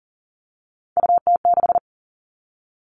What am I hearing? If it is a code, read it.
Morse code, heard as a keyed tone, about 40 wpm, 700 Hz. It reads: UT6
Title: All Cut Numbers - 3-Digit 40wpm